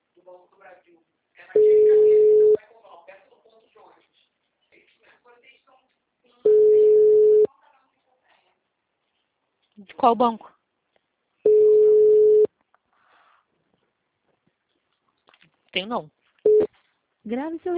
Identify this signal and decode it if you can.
Ringback tone (Europe, 425 Hz), with speech.